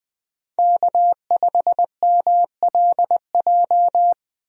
Morse code, heard as a keyed tone, about 20 wpm, 700 Hz. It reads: K5MLJ
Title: USA Call Signs - 5 Characters - Repeated 2x - 20wpm